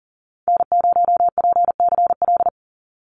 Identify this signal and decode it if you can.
Morse code, heard as a keyed tone, about 40 wpm, 700 Hz. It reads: N0PCL
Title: NA SOTA Activator Callsigns 40wpm